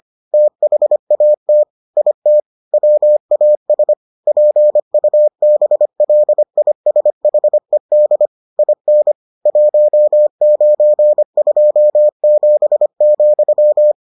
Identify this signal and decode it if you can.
Morse code, heard as a keyed tone, about 25 wpm, 600 Hz. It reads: THAT IT WAS PUBLISHED IN 1927,